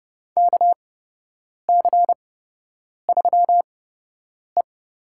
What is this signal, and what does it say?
Morse code, heard as a keyed tone, about 30 wpm, spaced as for 8 wpm, 700 Hz. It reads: KC3E
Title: USA Call Signs - 6 Characters - 8x Character Spacing - 30wpm